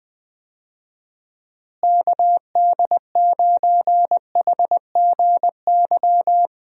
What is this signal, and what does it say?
Morse code, heard as a keyed tone, about 20 wpm, 700 Hz. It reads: KD9HGY
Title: USA Call Signs - 6 Characters - Repeated 2x - 20wpm